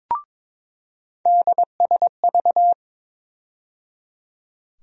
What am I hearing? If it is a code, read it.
Morse code, heard as a keyed tone, about 22 wpm, 700 Hz. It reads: DSV
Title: USA Call Sign Postfixes - 3 Letters 22wpm